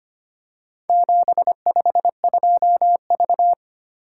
Morse code, heard as a keyed tone, about 25 wpm, 700 Hz. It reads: 752V
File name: practice / 25wpm / Four Letter-Number Combinations - 25wpm